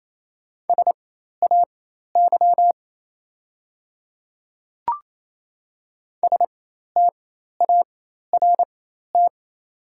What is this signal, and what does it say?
Morse code, heard as a keyed tone, about 28 wpm, spaced as for 13 wpm, 700 Hz. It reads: SAY START
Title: 4x Character Spacing - Top 300 Words - 28wpm